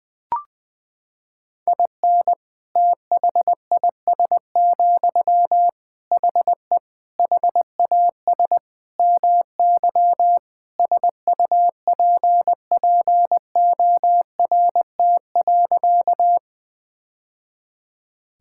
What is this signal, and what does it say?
Morse code, heard as a keyed tone, about 20 wpm, 700 Hz. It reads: IN THIS, HE HAS MY SUPPORT.
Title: Sentences from Top 1000 Words - Review Part 5 - 20wpm